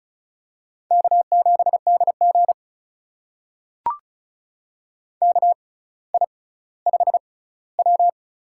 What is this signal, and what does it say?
Morse code, heard as a keyed tone, about 35 wpm, spaced as for 18 wpm, 700 Hz. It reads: K7DG KI5W
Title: USA Call Signs - 4 Characters - 6x Character Spacing - 35wpm